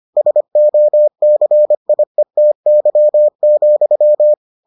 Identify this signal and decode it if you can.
Morse code, heard as a keyed tone, about 25 wpm, 600 Hz. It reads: SOCIETY,